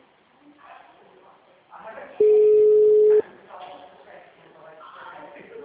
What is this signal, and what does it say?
Ringback tone (Europe, 425 Hz), with speech.
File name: out- -2011- - - .49955